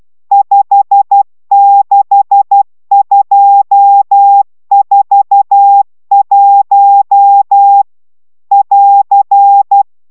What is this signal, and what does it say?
Morse code, heard as a keyed tone, about 12 wpm, 800 Hz. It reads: 56241 <AR>
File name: taviro tanulas / cw 69 22 10